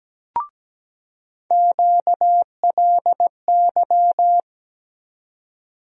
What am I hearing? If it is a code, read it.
Morse code, heard as a keyed tone, about 17 wpm, 700 Hz. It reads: QLY
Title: USA Call Sign Postfixes - 3 Letters 17wpm